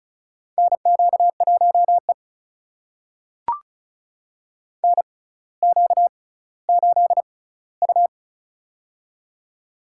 Morse code, heard as a keyed tone, about 36 wpm, spaced as for 18 wpm, 700 Hz. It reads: NQ1E NQ8U
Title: USA Call Signs - 4 Characters - 6x Character Spacing - 35wpm